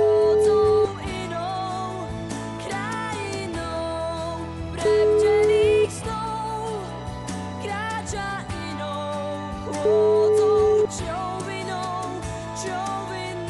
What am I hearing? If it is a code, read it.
Ringback tone (Europe, 425 Hz), with noise.